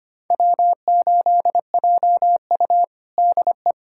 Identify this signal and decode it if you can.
Morse code, heard as a keyed tone, about 25 wpm, 700 Hz. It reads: W8JU DE
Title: USA Callsign DE Callsign - 4 Characters - 25wpm